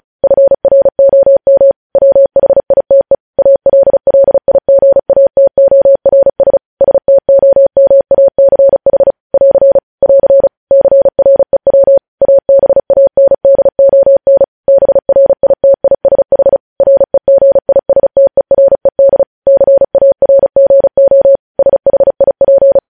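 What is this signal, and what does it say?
Morse code, heard as a keyed tone, about 36 wpm, 550 Hz. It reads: FROM WHITE ALLIGATORS STOMACH <AR> <AR> CREW ABANDON BRITISH REGISTERED CARGO SHIP
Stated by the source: Morse news headlines practice at 35.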